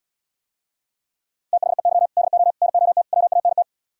Morse code, heard as a keyed tone, about 77 wpm, 700 Hz. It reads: THEY DO NOT WANT